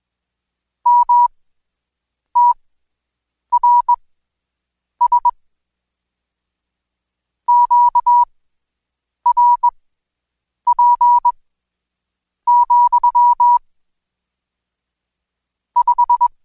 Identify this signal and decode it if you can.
Morse code, heard as a keyed tone, about 20 wpm, spaced as for 7 wpm, 970 Hz. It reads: MTRS QRP, 5
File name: Code Quick / AudioDownld2 / Seg112-Day17-DayDrill 17